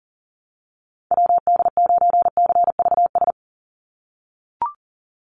Morse code, heard as a keyed tone, about 40 wpm, 700 Hz. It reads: WD9CVS
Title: USA Call Signs 40wpm